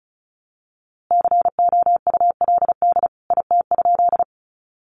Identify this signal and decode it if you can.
Morse code, heard as a keyed tone, about 34 wpm, 700 Hz. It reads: COULD IT?
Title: A Sound Of Thunder 35wpm